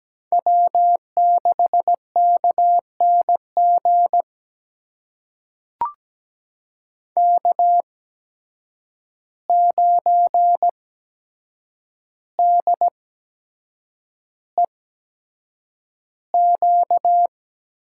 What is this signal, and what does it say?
Morse code, heard as a keyed tone, about 17 wpm, spaced as for 7 wpm, 700 Hz. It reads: W6KNG K9DEQ
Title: USA Call Signs - 5 Characters - 8x Character Spacing - 17wpm